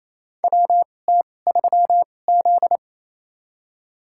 Morse code, heard as a keyed tone, about 28 wpm, spaced as for 20 wpm, 700 Hz. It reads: WT3Z
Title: USA Call Signs - 4 Characters - 2x Character Spacing - 28wpm